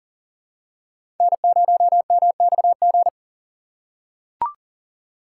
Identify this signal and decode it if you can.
Morse code, heard as a keyed tone, about 40 wpm, 700 Hz. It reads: N0MXG
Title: USA Call Signs - 5 Characters - 3x Character Spacing - 40wpm